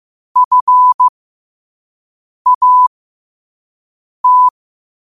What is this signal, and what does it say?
Morse code, heard as a keyed tone, about 15 wpm, spaced as for 5 wpm, 1000 Hz. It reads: FAT
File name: qotdfiles / 5wpm